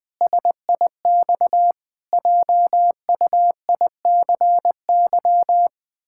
Morse code, heard as a keyed tone, about 19 wpm, 700 Hz. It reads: SIX JUICY
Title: Pangram Challenge v2 - 20wpm